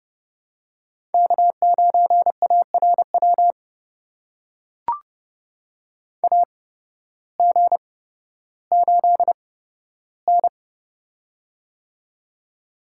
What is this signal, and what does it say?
Morse code, heard as a keyed tone, about 30 wpm, spaced as for 14 wpm, 700 Hz. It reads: K9ARW AG8N
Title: USA Call Signs - 8x Character Spacing - 30wpm